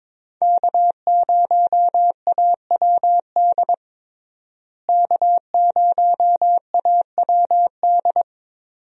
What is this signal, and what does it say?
Morse code, heard as a keyed tone, about 21 wpm, 700 Hz. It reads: K0AWD K0AWD
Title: USA Call Signs - 5 Characters - Repeated 2x - 22wpm